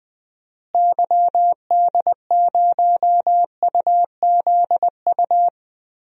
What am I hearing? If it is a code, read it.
Morse code, heard as a keyed tone, about 21 wpm, 700 Hz. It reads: YD0UZU
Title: DXCC-100 Call Signs - 6 Characters - Repeated 3x - 20wpm